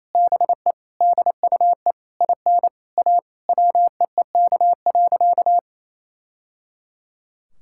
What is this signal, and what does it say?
Morse code, heard as a keyed tone, about 28 wpm, 700 Hz. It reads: BE DUE IN A WEEK.
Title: Sentences from Top 1000 Words - Review Part 6 - 28wpm